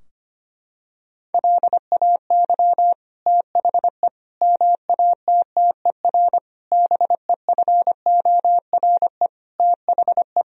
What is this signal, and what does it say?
Morse code, heard as a keyed tone, about 25 wpm, 700 Hz. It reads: LAY THE MATTER BEFORE THE
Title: Sentences from Top 1000 Words - Review Part 4 - 25wpm